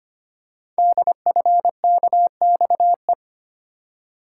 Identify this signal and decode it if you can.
Morse code, heard as a keyed tone, about 25 wpm, 700 Hz. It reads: DFKXE
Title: Code Groups - 5 Letters - 25wpm